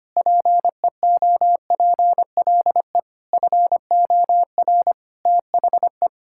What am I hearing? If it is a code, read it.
Morse code, heard as a keyed tone, about 25 wpm, 700 Hz. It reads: PEOPLE FOR THE
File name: Sets of 9 Words - 25wpm